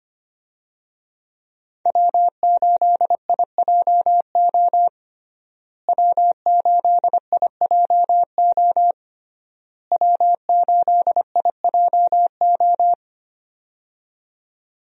Morse code, heard as a keyed tone, about 25 wpm, 700 Hz. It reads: W8IJO W8IJO W8IJO
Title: USA Call Signs - 5 Characters - Repeated 3x - 25wpm